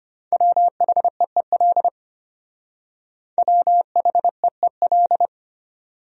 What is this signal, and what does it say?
Morse code, heard as a keyed tone, about 31 wpm, 700 Hz. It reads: WHEEL WHEEL
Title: Speed-Racing Top 500 Words 25wpm